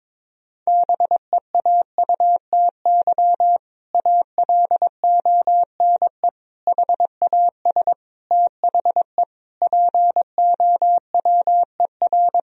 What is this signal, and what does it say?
Morse code, heard as a keyed tone, about 22 wpm, 700 Hz. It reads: BEAUTY ALONE HAS THE POWER
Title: Sentences from Top 1000 Words - Review Part 2 - 22wpm